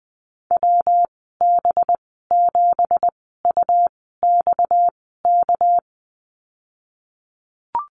Morse code, heard as a keyed tone, about 20 wpm, spaced as for 15 wpm, 700 Hz. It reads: WB7UXK
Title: 238 Lesson 38g - Letter q - Call Signs 20wpm